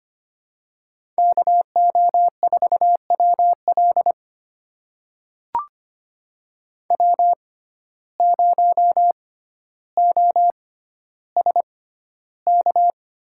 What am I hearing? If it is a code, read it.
Morse code, heard as a keyed tone, about 25 wpm, spaced as for 13 wpm, 700 Hz. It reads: KO4WL W0OSK
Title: USA Call Signs - 5 Characters - 6x Character Spacing - 25wpm